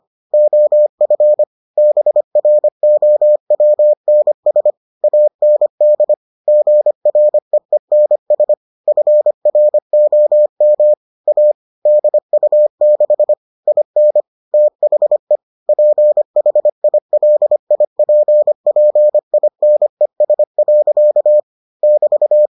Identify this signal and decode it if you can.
Morse code, heard as a keyed tone, about 25 wpm, 600 Hz. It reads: OF BROWNS AND GREENS FROM A DU6 IN THE PHILIPPINES. <BT>